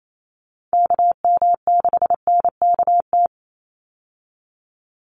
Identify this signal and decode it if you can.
Morse code, heard as a keyed tone, about 28 wpm, 700 Hz. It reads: KM6NKT
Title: USA Call Signs 28wpm